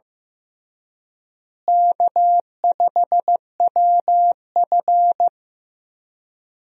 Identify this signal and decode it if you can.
Morse code, heard as a keyed tone, about 15 wpm, 700 Hz. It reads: K5WF
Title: USA Call Signs - 4 Characters - Repeated 3x - 15wpm